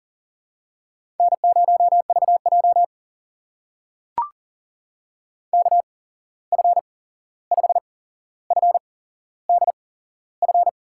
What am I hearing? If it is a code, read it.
Morse code, heard as a keyed tone, about 40 wpm, spaced as for 15 wpm, 700 Hz. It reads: N0VJ KF5FDF
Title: USA Call Signs - 8x Character Spacing - 40wpm